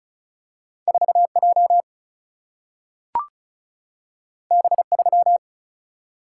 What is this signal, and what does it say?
Morse code, heard as a keyed tone, about 36 wpm, 700 Hz. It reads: 4 J B 3
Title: Single-Letter Single-Number 35wpm